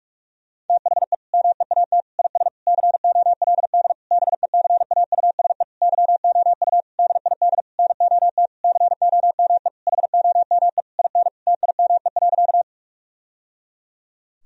Morse code, heard as a keyed tone, about 46 wpm, 700 Hz. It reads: THE MEAT IS COLD BECAUSE YOU DID NOT COME HOME IN TIME.
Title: Sentences from Top 1000 Words - Review Part 3 - 45wpm